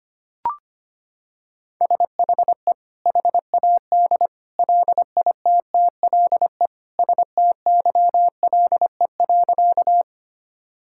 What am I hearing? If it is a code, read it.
Morse code, heard as a keyed tone, about 25 wpm, 700 Hz. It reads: SHE HAD LITTLE STYLE.
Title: Sentences from Top 1000 Words - Review Part 5 - 25wpm